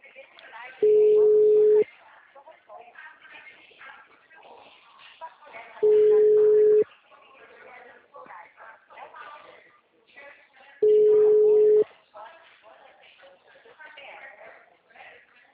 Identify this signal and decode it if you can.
Ringback tone (Europe, 425 Hz), with speech.